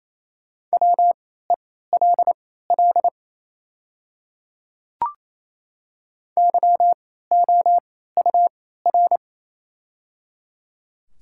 Morse code, heard as a keyed tone, about 28 wpm, spaced as for 16 wpm, 700 Hz. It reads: WELL YOUR
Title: 3x Character Spacing - Top 200 Words - 28wpm